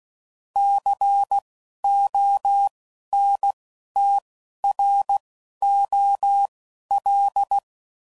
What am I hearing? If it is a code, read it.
Morse code, heard as a keyed tone, about 16 wpm, spaced as for 11 wpm, 800 Hz. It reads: CONTROL
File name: CW Code Groups @10WPM / Technical Subjects / OR Gate